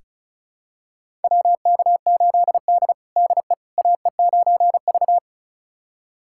Morse code, heard as a keyed tone, about 36 wpm, 700 Hz. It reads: WK8D DE AE9V
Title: USA Callsign DE Callsign - 4 Characters - 35wpm